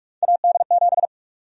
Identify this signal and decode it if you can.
Morse code, heard as a keyed tone, about 45 wpm, 700 Hz. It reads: AD7
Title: USA Call Sign Prefixes - Repeated 2x - 45wpm